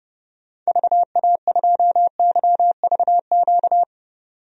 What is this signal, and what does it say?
Morse code, heard as a keyed tone, about 32 wpm, 700 Hz. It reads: VA2YVQ
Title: DXCC-100 Call Signs - 6 Characters - 30wpm